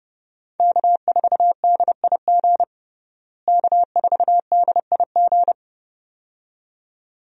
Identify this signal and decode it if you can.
Morse code, heard as a keyed tone, about 32 wpm, 700 Hz. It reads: K4DIG K4DIG
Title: USA Call Signs - 5 Characters - Repeated 2x - 30wpm